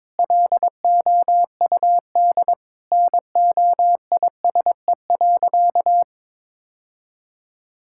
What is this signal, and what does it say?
Morse code, heard as a keyed tone, about 21 wpm, 700 Hz. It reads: LOUD NOISE.
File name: practice / 22wpm / Sentences with 3-Word Phrases - 22wpm